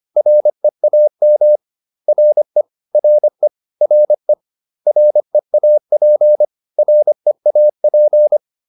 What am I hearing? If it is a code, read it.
Morse code, heard as a keyed tone, about 25 wpm, 600 Hz. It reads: REAM RE RE RE REAP REAP